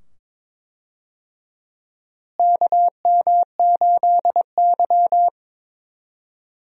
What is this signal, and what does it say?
Morse code, heard as a keyed tone, about 22 wpm, 700 Hz. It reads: KM8Y
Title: USA Call Signs - 4 Characters - Repeated 2x - 22wpm